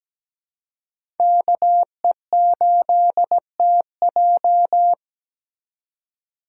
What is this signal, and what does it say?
Morse code, heard as a keyed tone, about 17 wpm, 700 Hz. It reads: KE8TJ
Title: USA Call Signs - 5 Characters - 2x Character Spacing - 17wpm